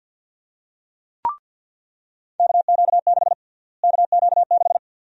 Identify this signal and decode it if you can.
Morse code, heard as a keyed tone, about 48 wpm, 700 Hz. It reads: KQ6 KQ6
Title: USA Call Sign Prefixes - Repeated 2x - 50wpm